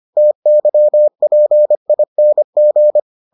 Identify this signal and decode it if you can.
Morse code, heard as a keyed tone, about 25 wpm, 600 Hz. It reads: TYPING